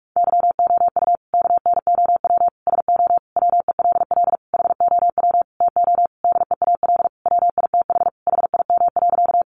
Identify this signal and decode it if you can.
Morse code, heard as a keyed tone, about 44 wpm, 700 Hz. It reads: YOU KNOW SO WELL HOW TO DEAL WITH HIM.
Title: Sentences from Top 800 Words 45wpm